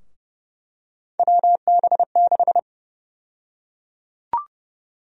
Morse code, heard as a keyed tone, about 30 wpm, 700 Hz. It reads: WB6
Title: USA Call Sign Prefixes - Repeated 2x - 30wpm